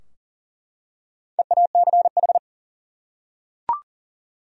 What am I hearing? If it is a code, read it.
Morse code, heard as a keyed tone, about 37 wpm, 700 Hz. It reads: EACH
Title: Top 100 Words 40wpm